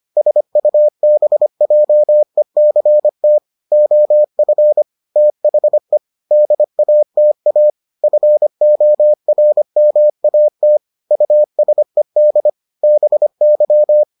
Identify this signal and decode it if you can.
Morse code, heard as a keyed tone, about 25 wpm, 600 Hz. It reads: SUBJECT OF THE DATA FORMAT USED BY